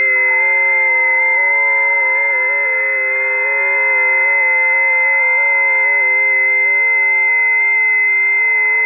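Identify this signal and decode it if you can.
FT8.
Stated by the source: FT8 on a busy 20 m band, a test recording.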